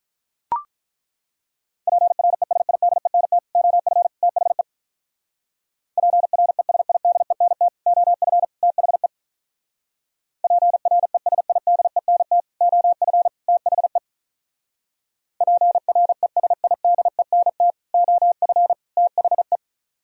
Morse code, heard as a keyed tone, about 42 wpm, 700 Hz. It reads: PRESIDENT OF THE PRESIDENT OF THE PRESIDENT OF THE PRESIDENT OF THE
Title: Speed-Racing - Sets of 3 Words 35wpm